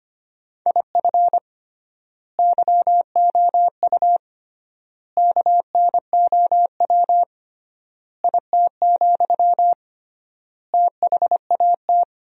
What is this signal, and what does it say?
Morse code, heard as a keyed tone, about 25 wpm, 700 Hz. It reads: IF YOU KNOW IT, THAT
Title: 3x Word-Spacing - Sentences from Top 400 Words - 25wpm